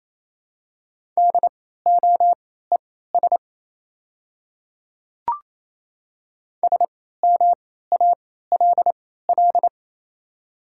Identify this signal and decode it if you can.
Morse code, heard as a keyed tone, about 28 wpm, spaced as for 16 wpm, 700 Hz. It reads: DOES SMALL
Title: 3x Character Spacing - Top 300 Words - 28wpm